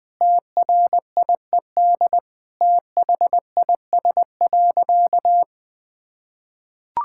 Morse code, heard as a keyed tone, about 20 wpm, 700 Hz. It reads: TRIED THIS.
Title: Sentences with 4-Word Phrases - Spoken Beginning - 20wpm